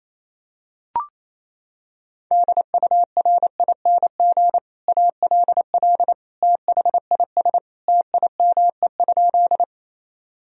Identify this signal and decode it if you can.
Morse code, heard as a keyed tone, about 28 wpm, 700 Hz. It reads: DURING ALL THIS TIME?
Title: Sentences from Top 1000 Words - Review Part 2 - 28wpm